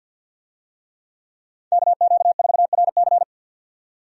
Morse code, heard as a keyed tone, about 51 wpm, 700 Hz. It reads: KQ4RC
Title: USA Call Signs - 5 Characters - 50wpm